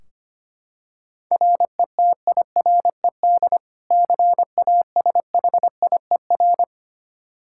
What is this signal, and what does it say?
Morse code, heard as a keyed tone, about 25 wpm, 700 Hz. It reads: RETIRED CASHIER
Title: Practice QSO - Exchange 3 - More Info - 25wpm